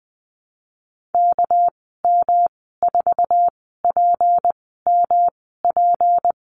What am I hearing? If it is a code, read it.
Morse code, heard as a keyed tone, about 20 wpm, spaced as for 14 wpm, 700 Hz. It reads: KM4PMP
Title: 213 Lesson 34g - Number 8 - Call Signs 20wpm